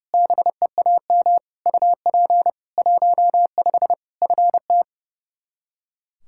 Morse code, heard as a keyed tone, about 30 wpm, 700 Hz. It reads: BEAM UP 15 FT
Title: K1USN SST QSO Practice 30wpm